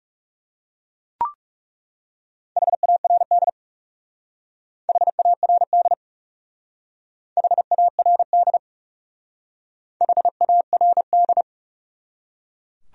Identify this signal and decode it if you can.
Morse code, heard as a keyed tone, about 37 wpm, 700 Hz. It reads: HARD HARD HARD HARD
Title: Speed-Racing Top 500 Words 30wpm